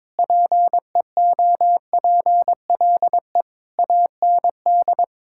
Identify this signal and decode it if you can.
Morse code, heard as a keyed tone, about 22 wpm, 700 Hz. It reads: PEOPLE AND
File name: Sentences with 9-Word Phrases - 22wpm